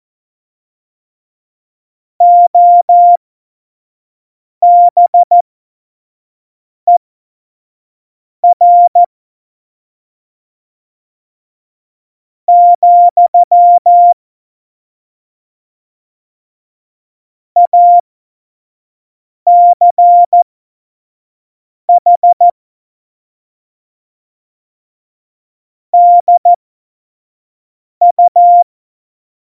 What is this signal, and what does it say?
Morse code, heard as a keyed tone, about 14 wpm, spaced as for 4 wpm, 700 Hz. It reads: OBER, ACH DU
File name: Morsekurs / Witze / Witz 14. Januar 2020, 25 F70